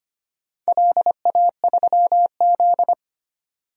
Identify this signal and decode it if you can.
Morse code, heard as a keyed tone, about 25 wpm, 700 Hz. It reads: LA3Z
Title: DXCC-100 Call Signs - 4 Characters - 25wpm